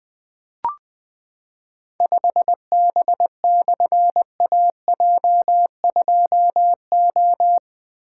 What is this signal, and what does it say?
Morse code, heard as a keyed tone, about 20 wpm, 700 Hz. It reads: 5B/AJ2O
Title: Difficult Call Signs 20wpm